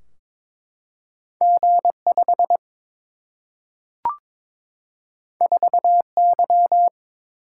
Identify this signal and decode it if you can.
Morse code, heard as a keyed tone, about 22 wpm, 700 Hz. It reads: G 5 4 Y
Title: Single-Letter Single-Number 22wpm